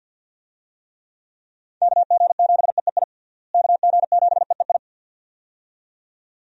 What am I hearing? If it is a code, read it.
Morse code, heard as a keyed tone, about 46 wpm, 700 Hz. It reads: KG7EEI KG7EEI
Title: USA Call Signs - 6 Characters - Repeated 2x - 50wpm